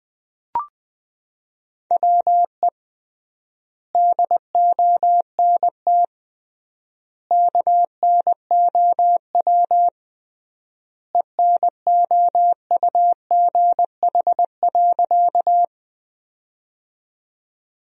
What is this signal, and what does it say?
Morse code, heard as a keyed tone, about 21 wpm, 700 Hz. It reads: WE DONT KNOW ENOUGH.
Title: 3x Word-Spacing - Sentences from Top 400 Words - 20wpm